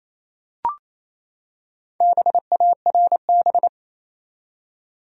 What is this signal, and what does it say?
Morse code, heard as a keyed tone, about 28 wpm, 700 Hz. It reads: BARB
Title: USA Common Names - Max 5 Letters - 28wpm